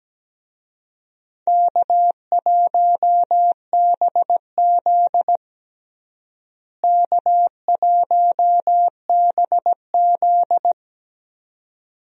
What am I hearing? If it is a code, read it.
Morse code, heard as a keyed tone, about 17 wpm, 700 Hz. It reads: K1BZ K1BZ
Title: USA Call Signs - 4 Characters - Repeated 2x - 17wpm